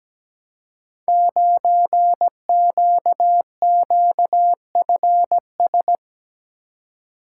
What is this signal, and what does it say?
Morse code, heard as a keyed tone, about 17 wpm, 700 Hz. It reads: 9QQFS
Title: Code Groups - 5 Letter:Numbers - 17wpm